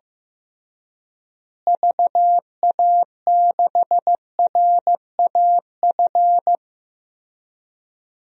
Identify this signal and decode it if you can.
Morse code, heard as a keyed tone, about 16 wpm, 700 Hz. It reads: VA6RAF
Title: DXCC-100 Call Signs - 6 Characters - Repeated 2x - 15wpm